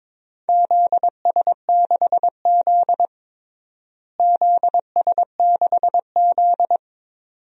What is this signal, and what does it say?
Morse code, heard as a keyed tone, about 22 wpm, 700 Hz. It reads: ZS6Z ZS6Z
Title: DXCC-100 Call Signs - 4 Characters - Repeated 2x - 22wpm